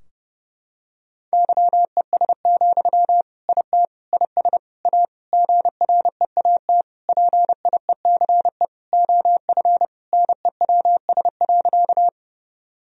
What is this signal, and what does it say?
Morse code, heard as a keyed tone, about 30 wpm, 700 Hz. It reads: YES, IT IS A GREAT PIECE OF NEWS.
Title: Sentences from Top 1000 Words - Review Part 2 - 30wpm